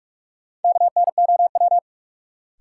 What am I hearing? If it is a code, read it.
Morse code, heard as a keyed tone, about 48 wpm, 700 Hz. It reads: KNOW
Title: Sentences from Top 100 Words - New Word Drill - 45wpm